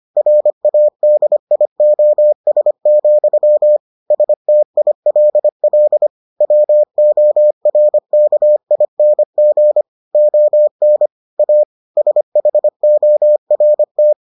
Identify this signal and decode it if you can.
Morse code, heard as a keyed tone, about 25 wpm, 600 Hz. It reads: RADIOS, STILL WORKING ON A SHORT